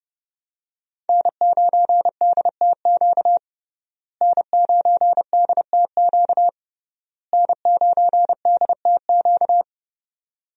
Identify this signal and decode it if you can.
Morse code, heard as a keyed tone, about 30 wpm, 700 Hz. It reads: N9DTQ N9DTQ N9DTQ
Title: USA Call Signs - 5 Characters - Repeated 3x - 30wpm